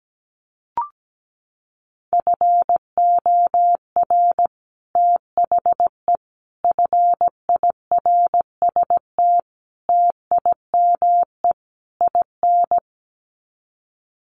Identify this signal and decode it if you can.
Morse code, heard as a keyed tone, about 17 wpm, 700 Hz. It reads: FOR THE FIRST TIME IN
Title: Sets of 5 Words 17wpm